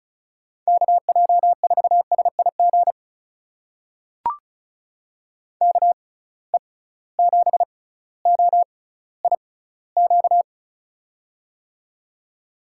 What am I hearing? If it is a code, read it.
Morse code, heard as a keyed tone, about 36 wpm, spaced as for 18 wpm, 700 Hz. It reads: KJ4SIG KE7OIQ
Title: USA Call Signs - 6 Characters - 6x Character Spacing - 35wpm